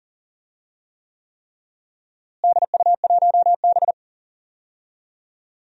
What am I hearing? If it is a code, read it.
Morse code, heard as a keyed tone, about 40 wpm, 700 Hz. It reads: DU1B
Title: DXCC-100 Call Signs - 4 Characters - Repeated 3x - 40wpm